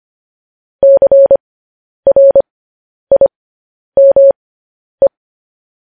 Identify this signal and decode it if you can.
Morse code, heard as a keyed tone, about 25 wpm, spaced as for 10 wpm, 550 Hz. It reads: CRIME
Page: news headlines in morse code at 10 wpm